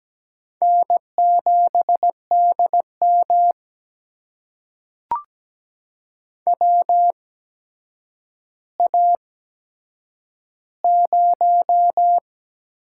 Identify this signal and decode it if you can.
Morse code, heard as a keyed tone, about 17 wpm, spaced as for 8 wpm, 700 Hz. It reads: N7DM WA0
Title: USA Call Signs - 4 Characters - 8x Character Spacing - 17wpm